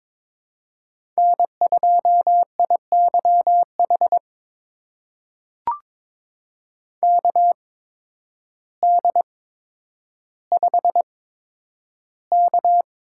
Morse code, heard as a keyed tone, about 22 wpm, spaced as for 10 wpm, 700 Hz. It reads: N2IYH KD5K
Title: USA Call Signs - 8x Character Spacing - 22wpm